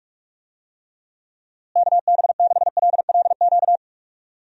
Morse code, heard as a keyed tone, about 41 wpm, 700 Hz. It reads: KB6LLQ
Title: USA Call Signs - 6 Characters - Repeated 3x - 45wpm